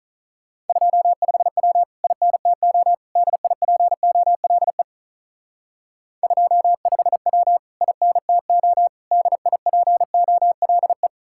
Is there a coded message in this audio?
Yes: Morse code from 0.7 to 11.1 s, heard as a keyed tone, about 45 wpm, 700 Hz. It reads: 25W INTO DIPOLE 25W INTO DIPOLE